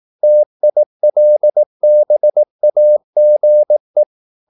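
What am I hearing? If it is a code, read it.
Morse code, heard as a keyed tone, about 18 wpm, 600 Hz. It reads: TILBAGE
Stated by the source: Slow (QRS) Morse practice at 18 wpm, CW med Gnister.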